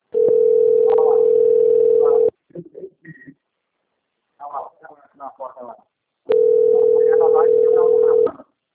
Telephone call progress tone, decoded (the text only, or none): ringback tone